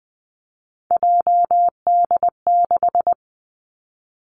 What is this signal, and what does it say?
Morse code, heard as a keyed tone, about 20 wpm, 700 Hz. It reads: JD6
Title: Three Challenging Letter-Number Combinations 20wpm